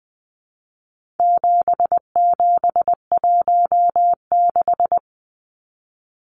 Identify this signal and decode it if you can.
Morse code, heard as a keyed tone, about 20 wpm, 700 Hz. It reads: 7716
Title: Numbers - 4-Digit 20wpm